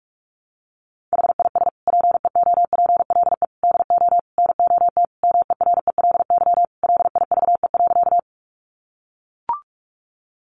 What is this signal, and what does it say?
Morse code, heard as a keyed tone, about 44 wpm, 700 Hz. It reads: HIS PEOPLE DO NOT MERELY LIVE.
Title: Sentences from Top 1400 Words 45wpm